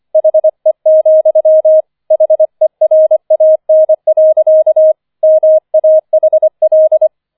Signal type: Morse code